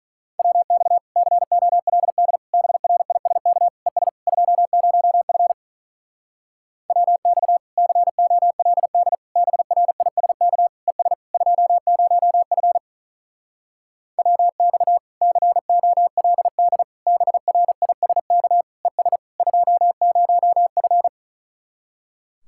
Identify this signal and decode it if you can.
Morse code, heard as a keyed tone, about 45 wpm, 700 Hz. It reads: WX COLD BRISK ES 20F WX COLD BRISK ES 20F WX COLD BRISK ES 20F